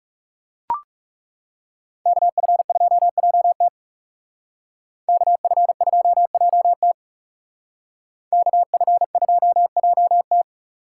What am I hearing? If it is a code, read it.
Morse code, heard as a keyed tone, about 36 wpm, 700 Hz. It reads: KF2JT KF2JT KF2JT
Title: Speed Racing - USA 2x2 Call Signs 30wpm